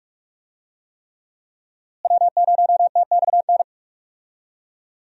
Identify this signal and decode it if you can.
Morse code, heard as a keyed tone, about 46 wpm, 700 Hz. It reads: W0TXN
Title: USA Call Signs - 5 Characters - Repeated 3x - 45wpm